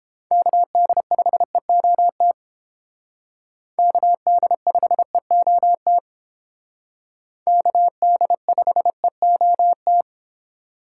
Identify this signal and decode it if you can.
Morse code, heard as a keyed tone, about 28 wpm, 700 Hz. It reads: KD5EOT KD5EOT KD5EOT